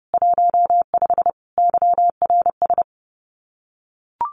Morse code, heard as a keyed tone, about 30 wpm, 700 Hz. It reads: 15 YRS
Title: Common QSO Phrases Encore 30wpm